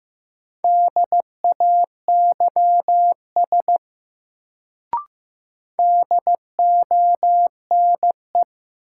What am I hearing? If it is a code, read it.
Morse code, heard as a keyed tone, about 15 wpm, 700 Hz. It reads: DAYS DONE
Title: Sets of 5 Words Encore - New Word Drill - 15wpm